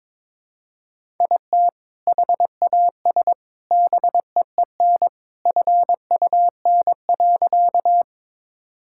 Morse code, heard as a keyed tone, about 22 wpm, 700 Hz. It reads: IT HAS BEEN FUN.
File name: practice / 22wpm / Sentences with 3-Word Phrases - 22wpm